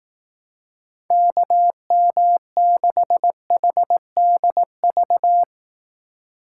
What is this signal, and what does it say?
Morse code, heard as a keyed tone, about 18 wpm, 700 Hz. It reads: KM6HDV
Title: Speed Racing - USA 2x3 Call Signs 15wpm